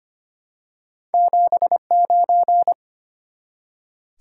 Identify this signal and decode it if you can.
Morse code, heard as a keyed tone, about 25 wpm, 700 Hz. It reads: 79
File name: Two Letter-Number Combinations - 25wpm